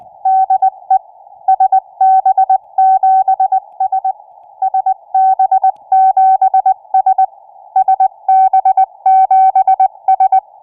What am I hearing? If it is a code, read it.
Morse code, heard as a keyed tone, about 20 wpm, 760 Hz. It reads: DE SB7S SB7S SB7S